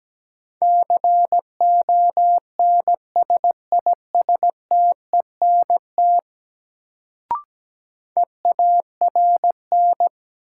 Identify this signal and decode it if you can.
Morse code, heard as a keyed tone, about 17 wpm, 700 Hz. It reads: CONSISTENT EARN
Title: Sentences from Top 1500 Words - New Word Drill Alt - 17wpm